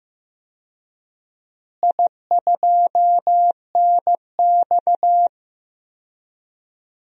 Morse code, heard as a keyed tone, about 16 wpm, 700 Hz. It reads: I2NX
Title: DXCC-100 Call Signs - 4 Characters - Repeated 2x - 15wpm